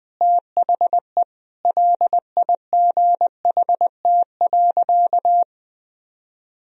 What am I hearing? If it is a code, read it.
Morse code, heard as a keyed tone, about 20 wpm, 700 Hz. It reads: THE LIGHT.
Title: Sentences from Top 1000 Words - Review Part 2 - 20wpm